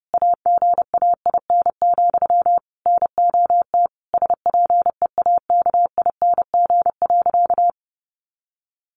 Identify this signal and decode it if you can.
Morse code, heard as a keyed tone, about 30 wpm, 700 Hz. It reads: AGAIN, NOT SPEAKING.
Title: A Sound Of Thunder 30wpm